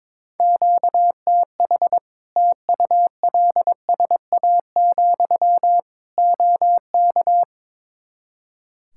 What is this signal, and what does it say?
Morse code, heard as a keyed tone, about 22 wpm, 700 Hz. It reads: QTH TULSA, OK
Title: Practice QSO - Exchange 1 - Beginning - 22wpm